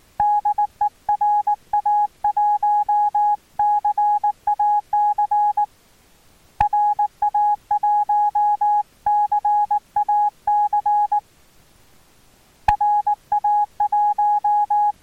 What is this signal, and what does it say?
Morse code, heard as a keyed tone, about 19 wpm, 830 Hz. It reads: DE RA1 CAC RA1 CAC RA1
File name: attachments / RX 28060 agc out res